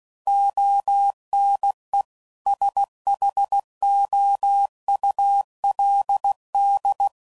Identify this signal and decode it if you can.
Morse code, heard as a keyed tone, about 16 wpm, 800 Hz. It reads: ONE SHOULD